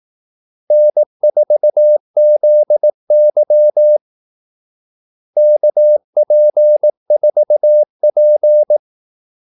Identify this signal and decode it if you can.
Morse code, heard as a keyed tone, about 18 wpm, 600 Hz. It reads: N4ZY KP4P
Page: Kaldesignaler 18 wpm